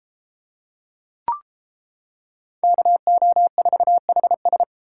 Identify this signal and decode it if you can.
Morse code, heard as a keyed tone, about 33 wpm, 700 Hz. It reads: KO4HS